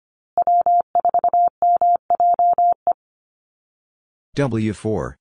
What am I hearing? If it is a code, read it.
Morse code, heard as a keyed tone, about 25 wpm, 700 Hz. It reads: W4MJE
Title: Call Signs Repeated 3x - 25wpm